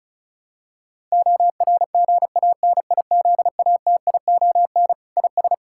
Morse code, heard as a keyed tone, about 34 wpm, 700 Hz. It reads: ORGANIZATION IS
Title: Sentences from Top 1000 Words - Review Part 5 - 35wpm